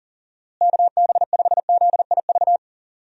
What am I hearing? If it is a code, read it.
Morse code, heard as a keyed tone, about 40 wpm, 700 Hz. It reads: KB5ZIV
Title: USA Call Signs - 6 Characters - Repeated 3x - 40wpm